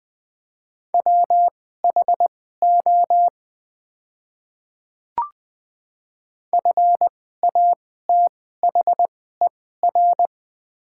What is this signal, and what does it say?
Morse code, heard as a keyed tone, about 21 wpm, spaced as for 15 wpm, 700 Hz. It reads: WHO FATHER
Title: 2x Character Spacing - Top 300 Words - 20wpm